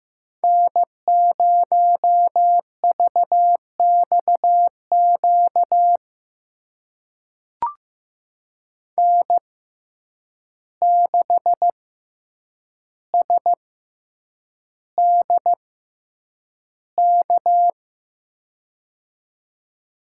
Morse code, heard as a keyed tone, about 15 wpm, spaced as for 8 wpm, 700 Hz. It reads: N0VXQ N6SDK
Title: USA Call Signs - 6x Character Spacing - 15wpm